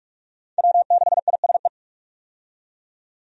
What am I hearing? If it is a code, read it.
Morse code, heard as a keyed tone, about 46 wpm, 700 Hz. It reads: W6ISE
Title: USA Call Signs - 5 Characters - 45wpm